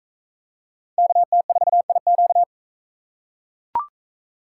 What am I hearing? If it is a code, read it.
Morse code, heard as a keyed tone, about 38 wpm, 700 Hz. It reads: KT4IQ